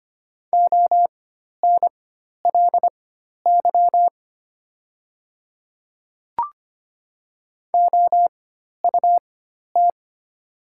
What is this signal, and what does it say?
Morse code, heard as a keyed tone, about 25 wpm, spaced as for 12 wpm, 700 Hz. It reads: ONLY OUT
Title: 4x Character Spacing - Top 200 Words - 25wpm